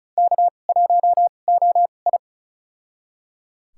Morse code, heard as a keyed tone, about 36 wpm, spaced as for 25 wpm, 700 Hz. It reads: K1OI
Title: USA Call Signs - 4 Characters - 2x Character Spacing - 35wpm